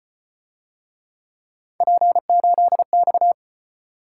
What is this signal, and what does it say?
Morse code, heard as a keyed tone, about 35 wpm, 700 Hz. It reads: P8X